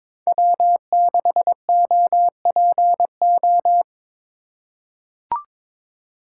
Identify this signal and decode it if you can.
Morse code, heard as a keyed tone, about 22 wpm, 700 Hz. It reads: W6OPO
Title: K1USN SST QSO Practice - 22wpm